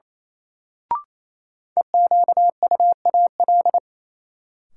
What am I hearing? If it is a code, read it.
Morse code, heard as a keyed tone, about 28 wpm, 700 Hz. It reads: EQUAL
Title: Sentences from Top 1400 Words - New Word Drill - 28wpm